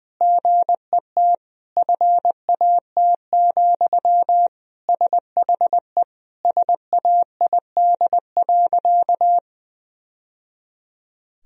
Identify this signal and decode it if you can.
Morse code, heard as a keyed tone, about 21 wpm, 700 Hz. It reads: GET FAT, SHE SAID.
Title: Sentences from Top 1000 Words - Review Part 4 - 20wpm